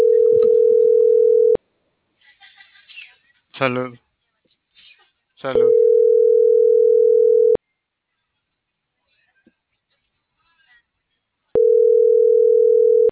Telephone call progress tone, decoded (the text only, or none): ringback tone